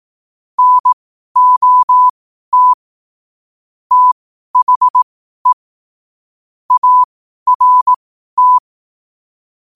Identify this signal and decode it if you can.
Morse code, heard as a keyed tone, about 18 wpm, spaced as for 12 wpm, 1000 Hz. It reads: NOT THE ART
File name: qotdfiles / 12wpm